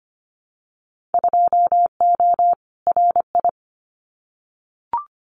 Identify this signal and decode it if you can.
Morse code, heard as a keyed tone, about 25 wpm, 700 Hz. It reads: 2O RI
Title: Winter Field Day Exchanges 25wpm